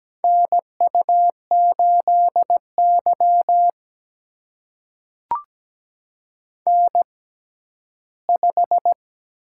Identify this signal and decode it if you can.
Morse code, heard as a keyed tone, about 17 wpm, spaced as for 12 wpm, 700 Hz. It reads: NU8Y N5
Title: USA Call Signs - 4 Characters - 6x Character Spacing - 17wpm